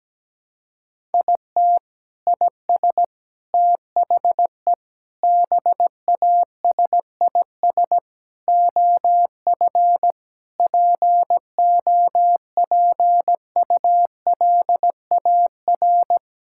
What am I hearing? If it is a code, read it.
Morse code, heard as a keyed tone, about 17 wpm, 700 Hz. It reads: IT IS THE BASIS OF POPULAR
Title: Sentences from Top 1000 Words - Review Part 5 - 17wpm